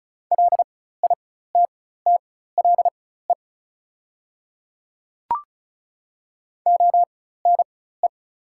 Morse code, heard as a keyed tone, about 36 wpm, spaced as for 16 wpm, 690 Hz. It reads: LITTLE ONE
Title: 4x Character Spacing - Top 200 Words - 35wpm